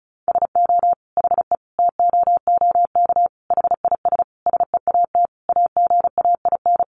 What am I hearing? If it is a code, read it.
Morse code, heard as a keyed tone, about 34 wpm, 700 Hz. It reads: SO HE TOOK HIS SEAT AGAIN
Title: Sentences from Top 1500 Words 35wpm